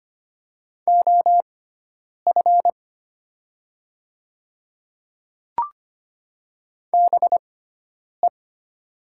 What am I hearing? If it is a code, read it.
Morse code, heard as a keyed tone, about 25 wpm, spaced as for 9 wpm, 700 Hz. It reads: OF BE
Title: 6x Character Spacing - Top 100 Words - 25wpm